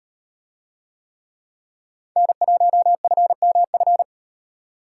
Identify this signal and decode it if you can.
Morse code, heard as a keyed tone, about 38 wpm, 700 Hz. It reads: N1FMF